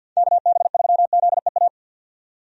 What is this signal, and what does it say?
Morse code, heard as a keyed tone, about 52 wpm, 700 Hz. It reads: KB3ZEA
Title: USA Call Signs - 6 Characters - 50wpm